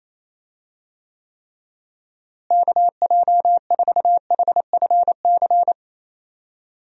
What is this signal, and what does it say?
Morse code, heard as a keyed tone, about 28 wpm, 700 Hz. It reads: KJ4HFC
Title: USA Call Signs - 6 Characters - Repeated 3x - 28wpm